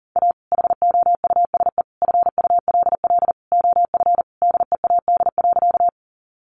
Morse code, heard as a keyed tone, about 40 wpm, 700 Hz. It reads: A HOUSE FULL OF DEAD.
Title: Sentences from Top 1500 Words 40wpm